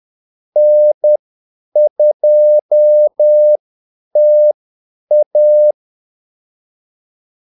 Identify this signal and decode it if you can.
Morse code, heard as a keyed tone, about 10 wpm, spaced as for 8 wpm, 600 Hz. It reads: N2TA
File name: Callsigns-10-8wpm